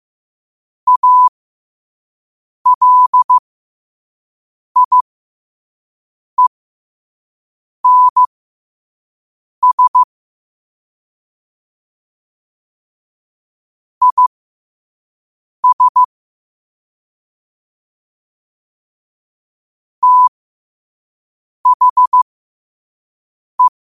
Morse code, heard as a keyed tone, about 15 wpm, spaced as for 5 wpm, 1000 Hz. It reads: ALIENS IS THE
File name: qotdfiles / 5wpm